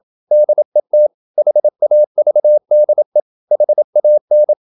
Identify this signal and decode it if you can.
Morse code, heard as a keyed tone, about 27 wpm, 600 Hz. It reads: DET HAVDE HAN
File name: Ferie QRQ 27wpm